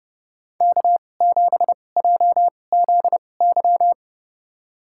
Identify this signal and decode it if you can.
Morse code, heard as a keyed tone, about 30 wpm, spaced as for 22 wpm, 700 Hz. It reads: K7JZY
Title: USA Call Signs - 5 Characters - 2x Character Spacing - 30wpm